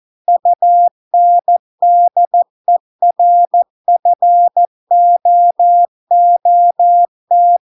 Morse code, heard as a keyed tone, about 14 wpm, 700 Hz. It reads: UNDERFOOT